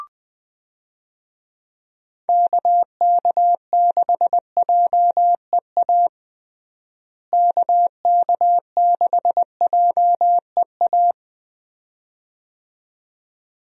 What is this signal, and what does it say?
Morse code, heard as a keyed tone, about 20 wpm, 700 Hz. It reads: KK6JEA KK6JEA
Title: USA Call Signs - 6 Characters - Repeated 2x - 20wpm